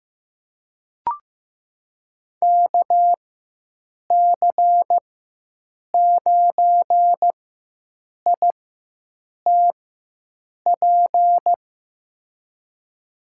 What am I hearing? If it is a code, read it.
Morse code, heard as a keyed tone, about 15 wpm, spaced as for 7 wpm, 700 Hz. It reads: KC9ITP
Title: USA Call Signs - 6 Characters - 4x Character Spacing - 15wpm